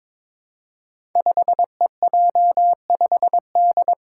Morse code, heard as a keyed tone, about 22 wpm, 700 Hz. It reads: 5EJ5D
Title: Five Challenging Letter-Number Combinations 22wpm